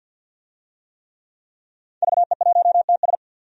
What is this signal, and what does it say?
Morse code, heard as a keyed tone, about 46 wpm, 700 Hz. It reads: VE1TS
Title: DXCC-100 Call Signs - 5 Characters - Repeated 3x - 50wpm